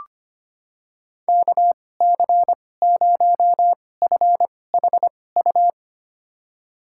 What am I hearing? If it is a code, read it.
Morse code, heard as a keyed tone, about 25 wpm, spaced as for 18 wpm, 700 Hz. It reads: KC0FHU
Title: USA Call Signs - 6 Characters - 2x Character Spacing - 25wpm